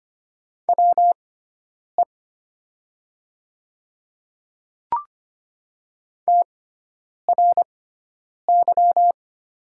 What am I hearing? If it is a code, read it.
Morse code, heard as a keyed tone, about 25 wpm, spaced as for 9 wpm, 700 Hz. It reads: WE TRY
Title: 6x Character Spacing - Top 200 Words - 25wpm